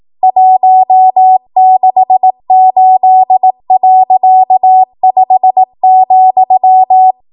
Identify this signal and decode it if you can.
Morse code, heard as a keyed tone, about 18 wpm, 750 Hz. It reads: 168.5,
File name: Archive / 18 WPM / 18WPM